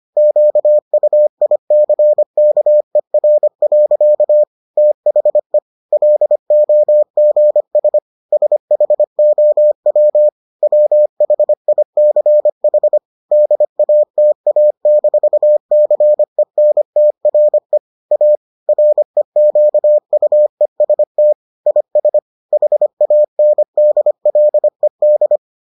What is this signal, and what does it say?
Morse code, heard as a keyed tone, about 25 wpm, 600 Hz. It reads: QUICKER. THE LOGS SHOW WHICH DATA-CENTRE A REQUEST IS HANDLED